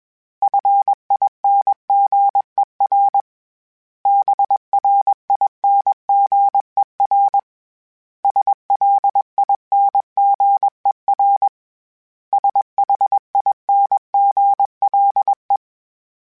Morse code, heard as a keyed tone, about 21 wpm, 800 Hz. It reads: FINGER BRINGER SLINGER SHINGLE
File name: Special INGs / Special INGs-mp3s / Hidden ings @20